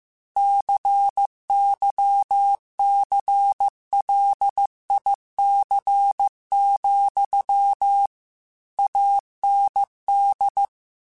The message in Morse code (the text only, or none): CYCLIC, AND